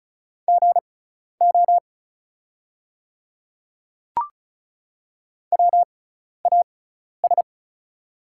Morse code, heard as a keyed tone, about 34 wpm, spaced as for 12 wpm, 690 Hz. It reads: GO WAS
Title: 6x Character Spacing - Top 100 Words - 35wpm